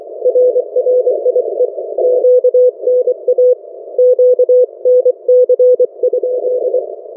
Morse code, heard as a keyed tone, about 24 wpm, 500 Hz. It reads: RRIE QNA QNC?
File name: RRIE-Net-2mins